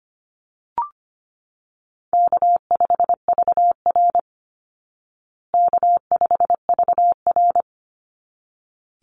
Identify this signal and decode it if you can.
Morse code, heard as a keyed tone, about 25 wpm, 700 Hz. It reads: K5VR K5VR
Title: Call Signs Repeated 3x - 25wpm